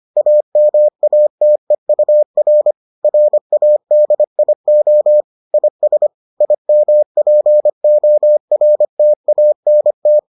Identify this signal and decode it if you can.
Morse code, heard as a keyed tone, about 25 wpm, 600 Hz. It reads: AMATEUR RADIO IS IMPORTANT